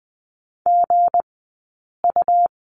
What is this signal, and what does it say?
Morse code, heard as a keyed tone, about 20 wpm, 700 Hz. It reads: G U
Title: 190 Lesson 31d - Letter v - Two Characters 20wpm